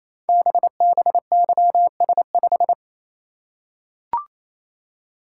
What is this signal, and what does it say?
Morse code, heard as a keyed tone, about 27 wpm, 700 Hz. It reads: BBYS5
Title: Code Groups - 5 Letter:Numbers - 28wpm